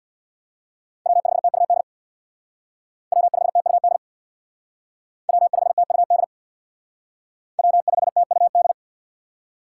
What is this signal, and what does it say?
Morse code, heard as a keyed tone, about 61 wpm, 700 Hz. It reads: W5TUD W5TUD W5TUD W5TUD
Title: Speed Racing - USA 1x3 Call Signs 50wpm